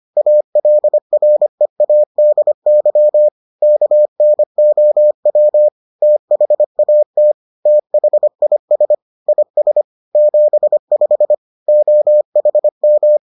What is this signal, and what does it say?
Morse code, heard as a keyed tone, about 25 wpm, 600 Hz. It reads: ALREADY KNOW THAT THIS IS 75 OHM